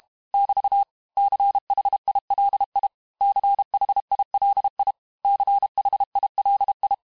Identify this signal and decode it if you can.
Morse code, heard as a keyed tone, about 32 wpm, 800 Hz. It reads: <BT> CHILI CHILI CHILI
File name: cours-du-mercredi / audios / cours-868-teste-MOTS-vitesse-30mpm